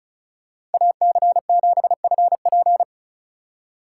Morse code, heard as a keyed tone, about 34 wpm, 700 Hz. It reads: AC7FP
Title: USA Call Signs - 5 Characters - 3x Character Spacing - 35wpm